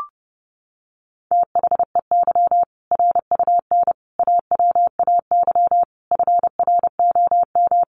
Morse code, heard as a keyed tone, about 30 wpm, 700 Hz. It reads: THEY RUN AWAY FROM
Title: Easy Phrases 30wpm